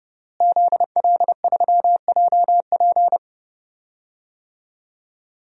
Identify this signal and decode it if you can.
Morse code, heard as a keyed tone, about 30 wpm, 700 Hz. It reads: ZL3JP
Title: DXCC-100 Call Signs - 5 Characters - Repeated 2x - 30wpm